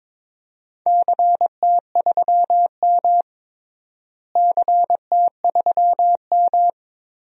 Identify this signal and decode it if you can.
Morse code, heard as a keyed tone, about 21 wpm, 700 Hz. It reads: CT3M CT3M
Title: DXCC-100 Call Signs - 4 Characters - Repeated 2x - 22wpm